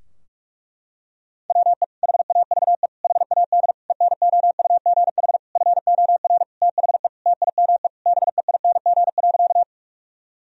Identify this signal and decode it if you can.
Morse code, heard as a keyed tone, about 46 wpm, 700 Hz. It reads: WE HAVE HAD ENOUGH FOR THE TIME BEING.
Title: Sentences from Top 1000 Words - Review Part 3 - 45wpm